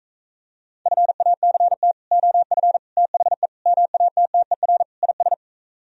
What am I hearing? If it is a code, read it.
Morse code, heard as a keyed tone, about 45 wpm, 700 Hz. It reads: FACT OF THE MATTER IS